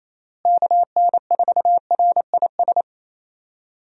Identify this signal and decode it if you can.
Morse code, heard as a keyed tone, about 28 wpm, 700 Hz. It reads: KN4RIS
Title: USA Call Signs - 6 Characters - Repeated 3x - 28wpm